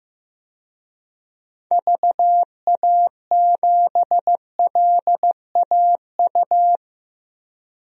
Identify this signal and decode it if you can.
Morse code, heard as a keyed tone, about 15 wpm, 700 Hz. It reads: VA7LAU
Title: DXCC-100 Call Signs - 6 Characters - 15wpm